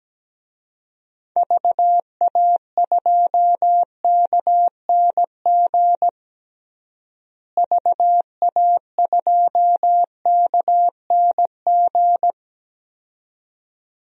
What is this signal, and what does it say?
Morse code, heard as a keyed tone, about 17 wpm, 700 Hz. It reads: VA2KNG VA2KNG
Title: DXCC-100 Call Signs - 6 Characters - Repeated 2x - 17wpm